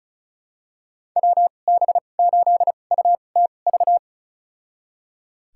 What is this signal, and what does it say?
Morse code, heard as a keyed tone, about 34 wpm, spaced as for 25 wpm, 700 Hz. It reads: WB8UTV
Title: USA Call Signs - 6 Characters - 2x Character Spacing - 35wpm